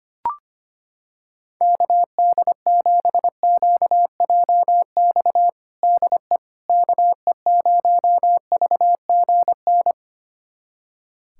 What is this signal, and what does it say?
Morse code, heard as a keyed tone, about 25 wpm, 700 Hz. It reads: KD7QJX DE KE0VGN
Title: USA Callsign DE Callsign 25wpm